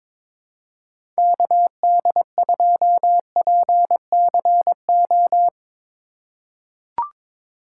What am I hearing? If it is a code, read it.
Morse code, heard as a keyed tone, about 22 wpm, 700 Hz. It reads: KD2PCO
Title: USA Call Signs - 6 Characters - 4x Character Spacing - 22wpm